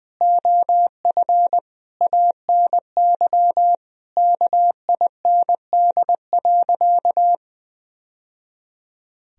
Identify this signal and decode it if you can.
Morse code, heard as a keyed tone, about 20 wpm, 700 Hz. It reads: OF ANY KIND.
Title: Sentences from Top 1000 Words - Review Part 4 - 20wpm